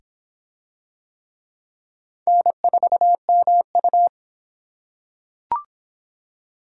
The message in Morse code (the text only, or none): N4MU